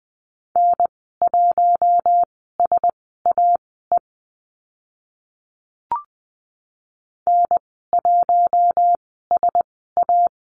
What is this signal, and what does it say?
Morse code, heard as a keyed tone, about 20 wpm, spaced as for 15 wpm, 700 Hz. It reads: N1SAE N1SA
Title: 042 Lesson 8g - Number 1 - Call Signs 20wpm